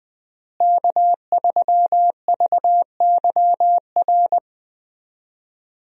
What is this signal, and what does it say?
Morse code, heard as a keyed tone, about 20 wpm, 700 Hz. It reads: K3VYR
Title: USA Call Signs - 5 Characters - Repeated 3x - 20wpm